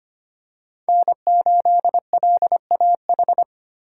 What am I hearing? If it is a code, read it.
Morse code, heard as a keyed tone, about 25 wpm, 700 Hz. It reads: N8LAH
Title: USA Call Signs - 5 Characters - 3x Character Spacing - 25wpm